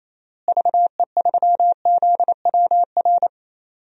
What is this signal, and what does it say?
Morse code, heard as a keyed tone, about 28 wpm, 700 Hz. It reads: VE3ZWR
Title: DXCC-100 Call Signs - 6 Characters - Repeated 3x - 28wpm